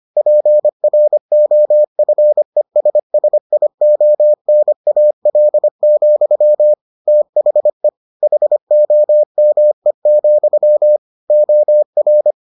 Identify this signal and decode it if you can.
Morse code, heard as a keyed tone, about 25 wpm, 600 Hz. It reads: PROFESSIONAL, THE HOME, OR